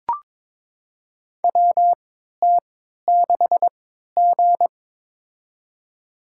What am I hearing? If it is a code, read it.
Morse code, heard as a keyed tone, about 21 wpm, spaced as for 12 wpm, 700 Hz. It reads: WT6G
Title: USA Call Signs - 4 Characters - 3x Character Spacing - 22wpm